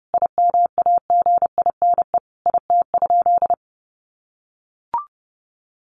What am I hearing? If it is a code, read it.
Morse code, heard as a keyed tone, about 30 wpm, 700 Hz. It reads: IMAGINE IT?
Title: Sentences from Top 1400 Words 30wpm